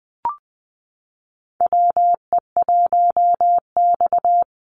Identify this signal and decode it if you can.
Morse code, heard as a keyed tone, about 20 wpm, 700 Hz. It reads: WE1X
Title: USA Call Signs 20wpm